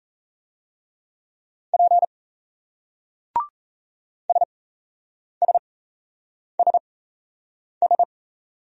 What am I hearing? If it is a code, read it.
Morse code, heard as a keyed tone, about 36 wpm, spaced as for 8 wpm, 700 Hz. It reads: P SSSS